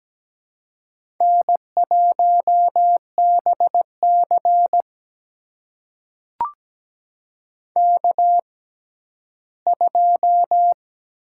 Morse code, heard as a keyed tone, about 17 wpm, spaced as for 12 wpm, 700 Hz. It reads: N1BC K2
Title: USA Call Signs - 6x Character Spacing - 17wpm